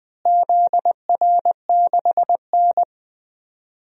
Morse code, heard as a keyed tone, about 20 wpm, 700 Hz. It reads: ZR6N
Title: DXCC-100 Call Signs - 4 Characters - Repeated 3x - 20wpm